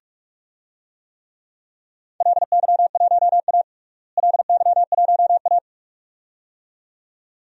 Morse code, heard as a keyed tone, about 47 wpm, 700 Hz. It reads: LY1A LY1A
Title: DXCC-100 Call Signs - 4 Characters - Repeated 2x - 45wpm